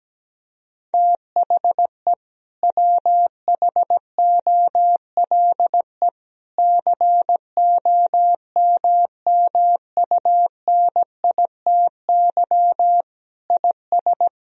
Morse code, heard as a keyed tone, about 17 wpm, 700 Hz. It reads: THE WHOLE COMMUNITY IS
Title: Sentences from Top 1000 Words - Review Part 4 - 17wpm